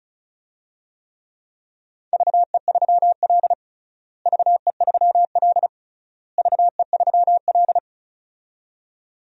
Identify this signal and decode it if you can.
Morse code, heard as a keyed tone, about 36 wpm, 700 Hz. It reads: VE3L VE3L VE3L
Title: DXCC-100 Call Signs - 4 Characters - Repeated 3x - 35wpm